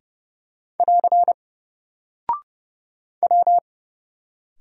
Morse code, heard as a keyed tone, about 30 wpm, 700 Hz. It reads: <AR> W
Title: Single Character (Letter:Number:Punctuation:Prosign) Alt 30wpm